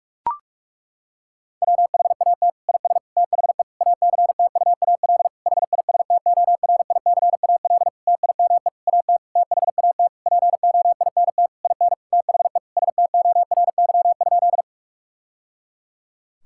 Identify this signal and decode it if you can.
Morse code, heard as a keyed tone, about 41 wpm, 700 Hz. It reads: WHAT IS THE ACTUAL HISTORICAL TIME AT THAT POINT IN THE STORY?
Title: Sentences from Top 1000 Words - Review Part 5 - 45wpm